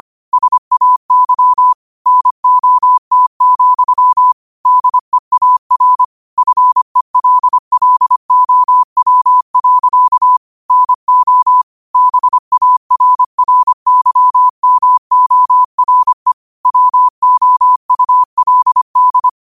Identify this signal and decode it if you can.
Morse code, heard as a keyed tone, about 25 wpm, 1000 Hz. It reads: SAY NOT, DEAR FELLOW. NO BARRYMORE WOULD
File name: qotdfiles / 25wpm